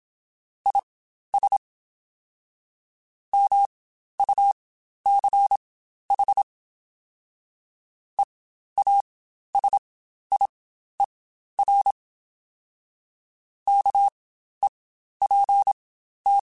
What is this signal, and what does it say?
Morse code, heard as a keyed tone, about 27 wpm, spaced as for 12 wpm, 800 Hz. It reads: IS MUCH EASIER KEPT
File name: Quotes Spaced / Quotes Spaced-mp3s / Character sp @25